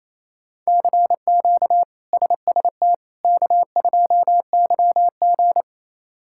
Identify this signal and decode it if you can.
Morse code, heard as a keyed tone, about 28 wpm, 700 Hz. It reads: CQ SST K2YG
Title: K1USN SST QSO Practice - 28wpm